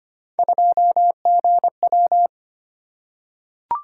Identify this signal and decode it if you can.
Morse code, heard as a keyed tone, about 25 wpm, 700 Hz. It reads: 2GW
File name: Three Letter-Number Combinations - 25wpm